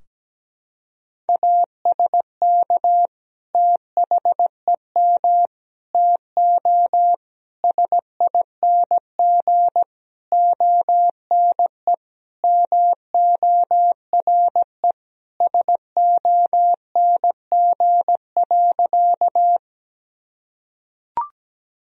Morse code, heard as a keyed tone, about 17 wpm, 700 Hz. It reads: ASK THEM TO SING ONE MORE SONG.
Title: Sentences from Top 1000 Words - Review Part 2 - 17wpm